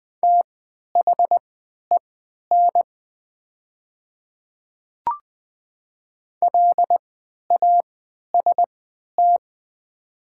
Morse code, heard as a keyed tone, about 20 wpm, spaced as for 11 wpm, 700 Hz. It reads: THEN LAST
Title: 3x Character Spacing - Top 300 Words - 20wpm